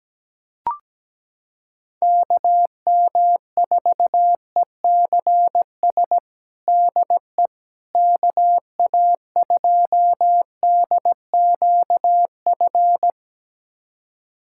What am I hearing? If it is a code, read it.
Morse code, heard as a keyed tone, about 17 wpm, 700 Hz. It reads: KM4ECS DE KA2DQF
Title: USA Callsign DE Callsign - 6 Characters - 17wpm